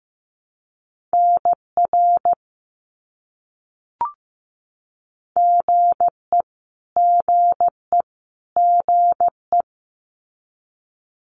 Morse code, heard as a keyed tone, about 15 wpm, 700 Hz. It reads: NR GE GE GE
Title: Abbreviations Repeated 3x - 15wpm